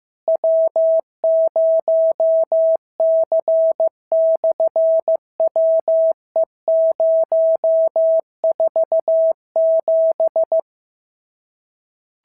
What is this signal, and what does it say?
Morse code, heard as a keyed tone, about 15 wpm, 650 Hz. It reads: W0C/WE047
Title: SOTA QSO Practice 15wpm